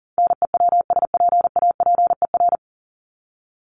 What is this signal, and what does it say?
Morse code, heard as a keyed tone, about 40 wpm, 700 Hz. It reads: NEWSPAPER
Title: Top 751-1000 Words 40wpm